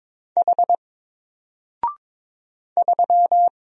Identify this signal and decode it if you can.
Morse code, heard as a keyed tone, about 22 wpm, 700 Hz. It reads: H 3
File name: Single-Letter-Number-22wpm